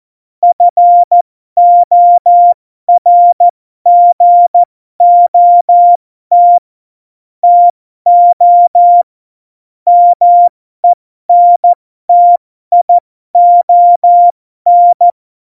Morse code, heard as a keyed tone, about 14 wpm, spaced as for 12 wpm, 700 Hz. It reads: FORGOT TO MENTION
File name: Morsekurs / Witze